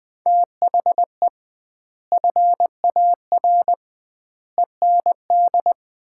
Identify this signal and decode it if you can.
Morse code, heard as a keyed tone, about 20 wpm, 700 Hz. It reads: THE FAR END
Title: 2x Word-Spacing - Sets of 5 Words - 20wpm